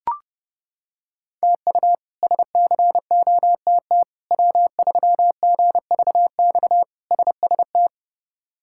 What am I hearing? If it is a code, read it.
Morse code, heard as a keyed tone, about 31 wpm, 700 Hz. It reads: TU SCOTT W3GVX SST
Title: K1USN SST QSO Practice - 30wpm